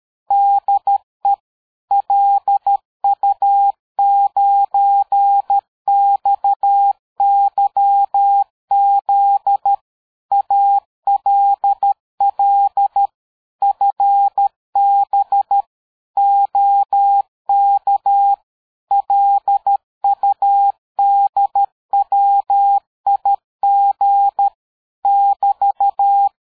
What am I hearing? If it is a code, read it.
Morse code, heard as a keyed tone, about 13 wpm, 800 Hz. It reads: DE LU9XYZ ALL FB OK LUDWIG <BT>
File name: dk5ke / QSO 60BpM